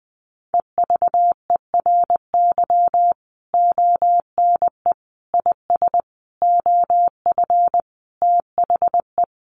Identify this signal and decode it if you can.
Morse code, heard as a keyed tone, about 20 wpm, 700 Hz. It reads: EVERY ONE IS OF THE
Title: Sentences from Top 1200 Words 20wpm